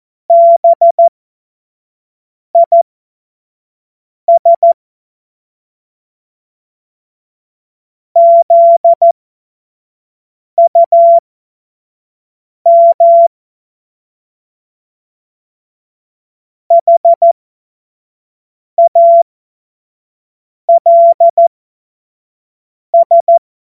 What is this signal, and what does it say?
Morse code, heard as a keyed tone, about 14 wpm, spaced as for 5 wpm, 680 Hz. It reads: BIS ZUM HALS
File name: Morsekurs / Witze / Witz 02. August 2022 25 F70